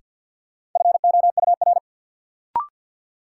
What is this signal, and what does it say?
Morse code, heard as a keyed tone, about 52 wpm, 700 Hz. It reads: FOUR
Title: Sentences from Top 300 Words - New Word Drill - 50wpm